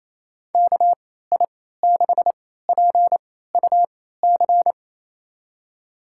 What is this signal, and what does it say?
Morse code, heard as a keyed tone, about 28 wpm, spaced as for 16 wpm, 700 Hz. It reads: KI6PUC
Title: USA Call Signs - 6 Characters - 3x Character Spacing - 28wpm